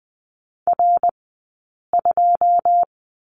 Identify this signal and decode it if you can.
Morse code, heard as a keyed tone, about 20 wpm, 700 Hz. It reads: R 2
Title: 120 Lesson 20d - Number 3 - Two Characters 20wpm